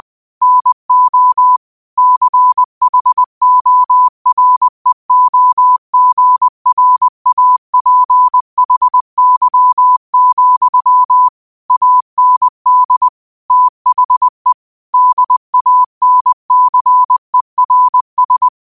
Morse code, heard as a keyed tone, about 20 wpm, 1000 Hz. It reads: NO CHOREOGRAPHY, AND THE DANCERS
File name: qotdfiles / 20wpm